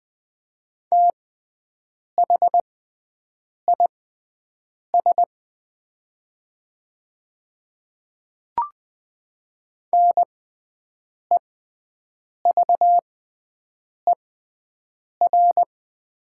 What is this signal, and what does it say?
Morse code, heard as a keyed tone, about 20 wpm, spaced as for 7 wpm, 700 Hz. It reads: THIS NEVER
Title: 6x Character Spacing - Top 300 Words - 20wpm